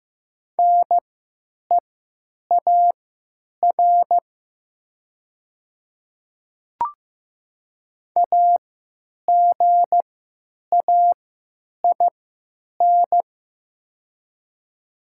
Morse code, heard as a keyed tone, about 15 wpm, spaced as for 8 wpm, 700 Hz. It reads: NEAR AGAIN
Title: 3x Character Spacing - Top 300 Words - 15wpm